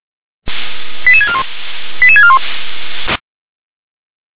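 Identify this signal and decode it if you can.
Five-tone selective call (ZVEI), heard as quick runs of five tones, tones of about 70 ms. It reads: 88641 88641
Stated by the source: ZVEI five-tone selective call received on a software-defined radio.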